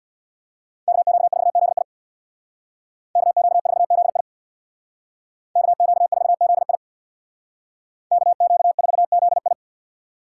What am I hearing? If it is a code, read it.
Morse code, heard as a keyed tone, about 60 wpm, 690 Hz. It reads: KQ4ZI KQ4ZI KQ4ZI KQ4ZI
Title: Speed Racing - USA 2x2 Call Signs 50wpm